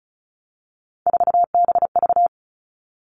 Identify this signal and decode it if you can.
Morse code, heard as a keyed tone, about 36 wpm, 700 Hz. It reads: 4BV
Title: Three Challenging Letter-Number Combinations 35wpm